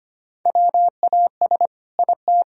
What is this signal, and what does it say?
Morse code, heard as a keyed tone, about 25 wpm, 700 Hz. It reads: WAS IT
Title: Sentences from Top 1000 Words - Review Part 4 - 25wpm